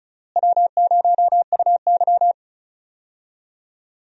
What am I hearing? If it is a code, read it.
Morse code, heard as a keyed tone, about 36 wpm, 700 Hz. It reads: W0UY
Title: USA Call Signs - 4 Characters - Repeated 3x - 35wpm